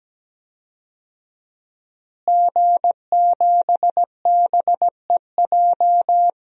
Morse code, heard as a keyed tone, about 17 wpm, 700 Hz. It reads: G7BEJ
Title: DXCC-100 Call Signs - 5 Characters - Repeated 2x - 17wpm